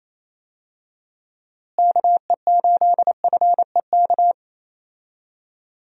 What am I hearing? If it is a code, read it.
Morse code, heard as a keyed tone, about 27 wpm, 700 Hz. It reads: KE8FEK
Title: USA Call Signs - 6 Characters - Repeated 3x - 28wpm